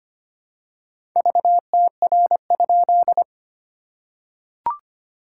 Morse code, heard as a keyed tone, about 25 wpm, 700 Hz. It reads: VTR?
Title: SOTA QSO Practice 25wpm